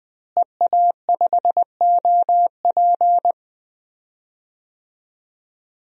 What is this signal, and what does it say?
Morse code, heard as a keyed tone, about 20 wpm, 700 Hz. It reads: EA5OP
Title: DXCC-100 Call Signs - 5 Characters - Repeated 2x - 20wpm